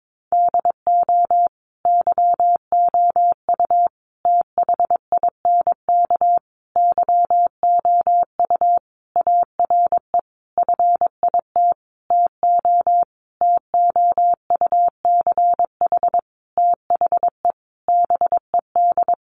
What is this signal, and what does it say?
Morse code, heard as a keyed tone, about 22 wpm, 700 Hz. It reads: DO YOU THINK YOU ARE FIT TO TOUCH THE BED
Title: Sentences from Top 1200 Words 22wpm